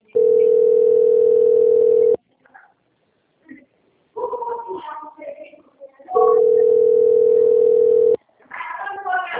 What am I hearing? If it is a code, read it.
Ringback tone (North America), 440 + 480 Hz, with speech.
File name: out- - - - - .17436